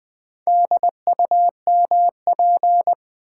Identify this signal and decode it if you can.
Morse code, heard as a keyed tone, about 20 wpm, 700 Hz. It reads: DUMP
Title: Sentences from Top 1300 Words - New Word Drill Alt - 20wpm